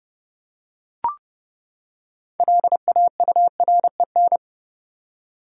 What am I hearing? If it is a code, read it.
Morse code, heard as a keyed tone, about 30 wpm, 700 Hz. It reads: LAUREN
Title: USA Common Names - 30wpm